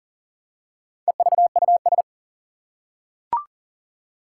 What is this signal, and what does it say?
Morse code, heard as a keyed tone, about 40 wpm, 700 Hz. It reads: EVUS
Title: Four Challenging Letter-Number Combinations 40wpm